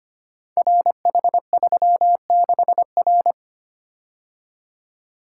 Morse code, heard as a keyed tone, about 25 wpm, 700 Hz. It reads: RH36R
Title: Code Groups - 5 Letter:Numbers - 25wpm